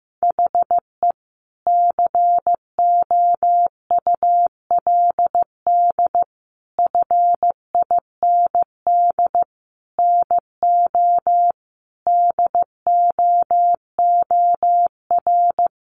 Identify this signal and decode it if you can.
Morse code, heard as a keyed tone, about 15 wpm, 700 Hz. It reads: HE COULD FIND NO DOOR
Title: Sentences from Top 1100 Words 15wpm